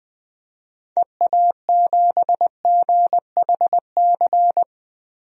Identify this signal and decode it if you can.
Morse code, heard as a keyed tone, about 21 wpm, 700 Hz. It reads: EA7GHC
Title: DXCC-100 Call Signs - 6 Characters - 20wpm